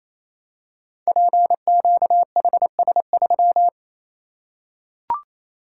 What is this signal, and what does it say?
Morse code, heard as a keyed tone, about 28 wpm, 700 Hz. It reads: PQHS3
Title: Code Groups - 5 Letter:Numbers - 28wpm